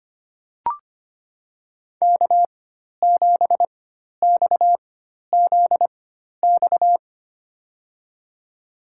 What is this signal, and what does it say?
Morse code, heard as a keyed tone, about 25 wpm, spaced as for 12 wpm, 700 Hz. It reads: K7XZX
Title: USA Call Signs - 4x Character Spacing - 25wpm